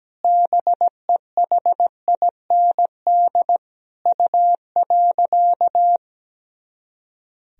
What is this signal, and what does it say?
Morse code, heard as a keyed tone, about 17 wpm, 700 Hz. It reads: BEHIND U.
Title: Sentences from Top 500 Words and Abbreviations 17wpm